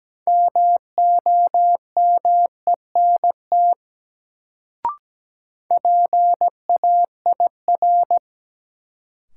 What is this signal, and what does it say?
Morse code, heard as a keyed tone, about 17 wpm, 700 Hz. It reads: MOMENT PAIR
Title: Sets of 3 Words - New Word Drill - 17wpm